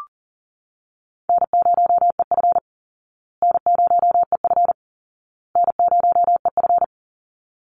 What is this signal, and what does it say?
Morse code, heard as a keyed tone, about 37 wpm, 700 Hz. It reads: N0EF N0EF N0EF
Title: Easy Call Signs Repeated 3x - 40wpm